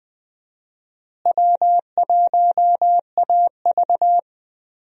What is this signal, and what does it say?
Morse code, heard as a keyed tone, about 20 wpm, 700 Hz. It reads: W1AV
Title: USA Call Signs - 4 Characters - Repeated 2x - 20wpm